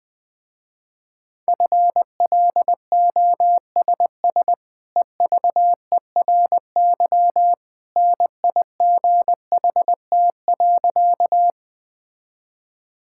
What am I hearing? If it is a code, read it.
Morse code, heard as a keyed tone, about 20 wpm, 700 Hz. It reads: FLOSS EVERY NIGHT.
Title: Easy Sentences with Hints - 20wpm